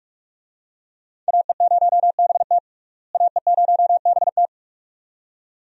Morse code, heard as a keyed tone, about 46 wpm, 700 Hz. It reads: AE0BT AE0BT
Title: USA Call Signs - 5 Characters - Repeated 2x - 45wpm